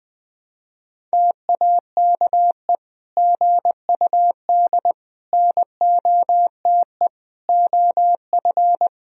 Morse code, heard as a keyed tone, about 21 wpm, 700 Hz. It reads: TAKE GUD NOTE OF
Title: Sentences from Top 500 Words and Abbreviations 20wpm